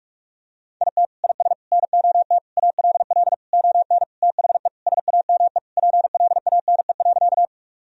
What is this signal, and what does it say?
Morse code, heard as a keyed tone, about 41 wpm, 700 Hz. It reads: IT IS NOT ALL ON THE SAME PLANE.
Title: Sentences from Top 1000 Words - Review Part 3 - 45wpm